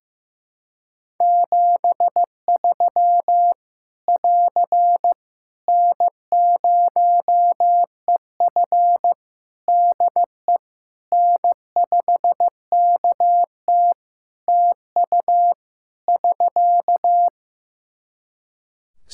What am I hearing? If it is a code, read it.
Morse code, heard as a keyed tone, about 15 wpm, 700 Hz. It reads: 73 <AR> N0EF DE N5KT TU <SK>
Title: Practice QSO - Exchange 4 - Ending - 15wpm